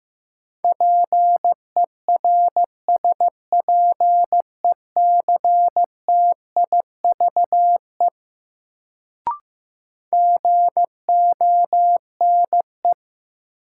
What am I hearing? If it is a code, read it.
Morse code, heard as a keyed tone, about 15 wpm, 700 Hz. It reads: PERSPECTIVE GONE
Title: Sentences from Top 1200 Words - New Word Drill - 15wpm